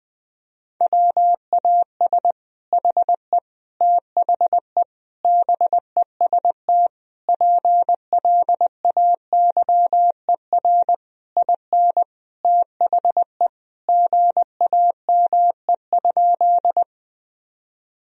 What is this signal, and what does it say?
Morse code, heard as a keyed tone, about 19 wpm, 700 Hz. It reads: WAS HE THE BEST PLAYER IN THE GAME?
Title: Sentences from Top 1000 Words - Review Part 4 - 20wpm